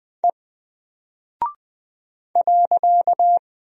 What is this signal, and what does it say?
Morse code, heard as a keyed tone, about 20 wpm, 700 Hz. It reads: E .
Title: Single Character (Letter:Number:Punctuation:Prosign) 20wpm